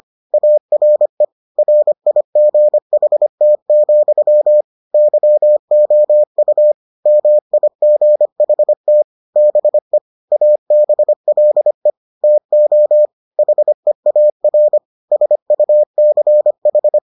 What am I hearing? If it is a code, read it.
Morse code, heard as a keyed tone, about 25 wpm, 600 Hz. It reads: ARE RIGHT, YOU MIGHT BE ABLE TO HEAR SUCH